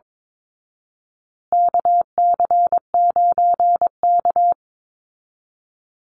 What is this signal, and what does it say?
Morse code, heard as a keyed tone, about 22 wpm, 700 Hz. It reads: KC9K
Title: Call Signs Repeated 2x - 22wpm